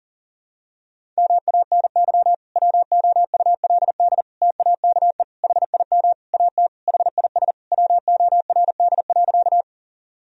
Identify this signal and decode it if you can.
Morse code, heard as a keyed tone, about 40 wpm, 700 Hz. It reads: MANY WOULD TAKE HIM AT HIS WORD.
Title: Sentences from Top 1000 Words - Review Part 1 - 40wpm